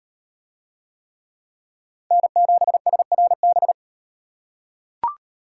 Morse code, heard as a keyed tone, about 35 wpm, 700 Hz. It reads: N7SRB